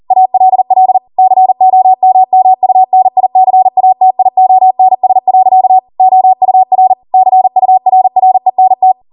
Morse code, heard as a keyed tone, about 40 wpm, 750 Hz. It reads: ALL COMMUNICATIONS. OUR CURRENT